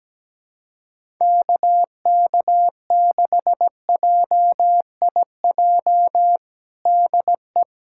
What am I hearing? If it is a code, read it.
Morse code, heard as a keyed tone, about 17 wpm, 700 Hz. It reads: KK6JIJ DE
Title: USA Callsign DE Callsign - 6 Characters - 17wpm